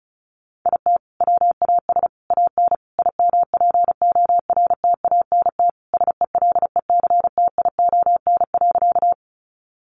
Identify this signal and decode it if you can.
Morse code, heard as a keyed tone, about 34 wpm, 700 Hz. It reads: IT WAS AN IMPORTANT SELECTION.
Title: Sentences from Top 1400 Words 35wpm